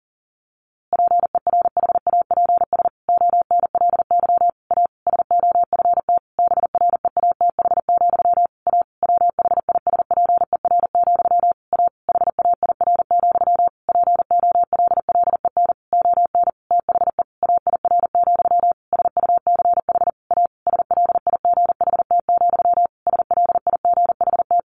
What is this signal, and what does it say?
Morse code, heard as a keyed tone, about 37 wpm, 700 Hz. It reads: PERHAPS ONLY A SOFT BREATH, A WHISPER, A HAIR, POLLEN ON THE AIR, SUCH A SLIGHT, SLIGHT
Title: A Sound Of Thunder 40wpm